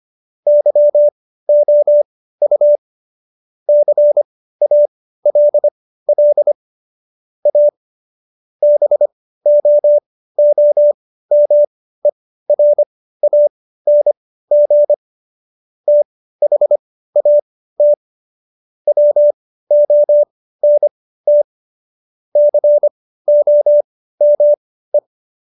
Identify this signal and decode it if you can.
Morse code, heard as a keyed tone, about 25 wpm, spaced as for 15 wpm, 590 Hz. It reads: YOU CALL A BOOMERANG THAT WONT COME
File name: Jokes-25-char-15-eff